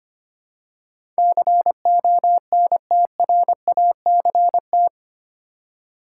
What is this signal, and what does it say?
Morse code, heard as a keyed tone, about 25 wpm, 700 Hz. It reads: CONTRACT
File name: 8 Letter Words - First 3 Letters Given - 25wpm